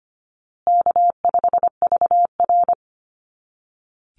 Morse code, heard as a keyed tone, about 25 wpm, 700 Hz. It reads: K5VR
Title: Easy Call Signs Repeated 3x - 25wpm